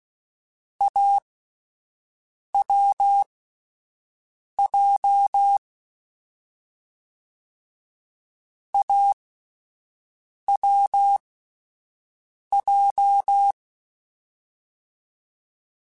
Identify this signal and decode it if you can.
Morse code, heard as a keyed tone, about 16 wpm, spaced as for 5 wpm, 800 Hz. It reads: AWJ AWJ
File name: Difficult Combos / Difficult Combos-mp3s / awj1 ndb6 gz7 @5